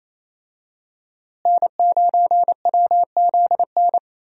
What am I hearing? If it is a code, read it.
Morse code, heard as a keyed tone, about 28 wpm, 700 Hz. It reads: N9WZN
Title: USA Call Signs - 5 Characters - Repeated 3x - 28wpm